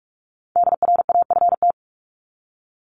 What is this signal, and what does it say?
Morse code, heard as a keyed tone, about 44 wpm, 700 Hz. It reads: DRAFT
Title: Top 1751-2000 Words 45wpm